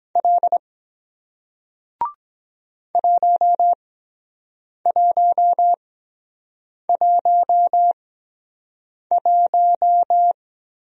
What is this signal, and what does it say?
Morse code, heard as a keyed tone, about 23 wpm, spaced as for 7 wpm, 700 Hz. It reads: L 1111